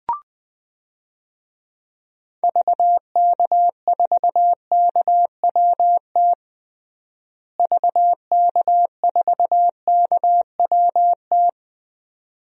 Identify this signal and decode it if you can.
Morse code, heard as a keyed tone, about 21 wpm, 700 Hz. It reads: VK4KWT VK4KWT
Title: DXCC-100 Call Signs - 6 Characters - Repeated 2x - 20wpm